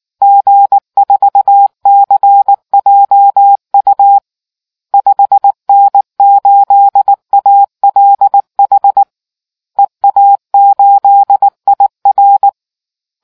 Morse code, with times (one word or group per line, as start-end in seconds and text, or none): 0.2-4.2 G4CJU
4.9-9.0 5N8ALH
9.8-12.5 EA8IR